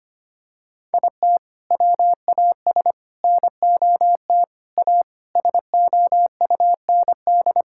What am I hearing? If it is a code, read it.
Morse code, heard as a keyed tone, about 25 wpm, 700 Hz. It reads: IT WAS NOT A SOUND
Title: Sentences from Top 1000 Words - Review Part 6 - 25wpm